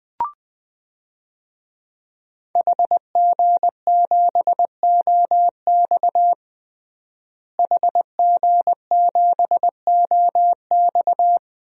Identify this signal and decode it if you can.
Morse code, heard as a keyed tone, about 20 wpm, 700 Hz. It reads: HG7OX HG7OX
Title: DXCC-100 Call Signs - 5 Characters - Repeated 2x - 20wpm